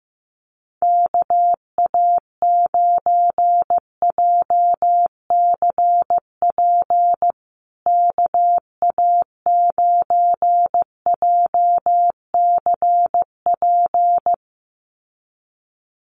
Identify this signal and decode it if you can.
Morse code, heard as a keyed tone, about 15 wpm, 700 Hz. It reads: KA9JCP KA9JCP
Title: USA Call Signs Repeated 2x 15wpm